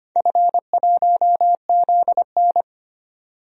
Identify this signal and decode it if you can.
Morse code, heard as a keyed tone, about 25 wpm, 700 Hz. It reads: F1ZN
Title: DXCC-100 Call Signs - 4 Characters - 25wpm